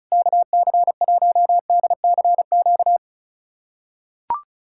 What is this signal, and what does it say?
Morse code, heard as a keyed tone, about 34 wpm, 690 Hz. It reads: KC1DCQ
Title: USA Callsign DE Callsign 35wpm